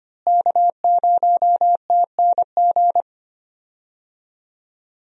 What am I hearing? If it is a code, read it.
Morse code, heard as a keyed tone, about 25 wpm, 700 Hz. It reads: K0TNG
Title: USA Call Signs - 5 Characters - Repeated 2x - 25wpm